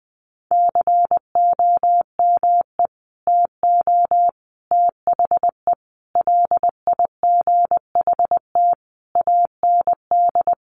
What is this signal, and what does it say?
Morse code, heard as a keyed tone, about 19 wpm, 700 Hz. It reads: COME TO THE LIGHT AND
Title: Sentences from Top 1300 Words 20wpm